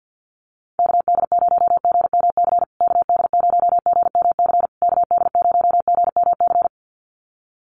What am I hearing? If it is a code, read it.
Morse code, heard as a keyed tone, about 50 wpm, 700 Hz. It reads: KD0GMC KD0GMC KD0GMC
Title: USA Call Signs Repeated 3x 50wpm